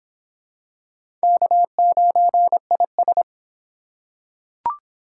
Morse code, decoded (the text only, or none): K9IS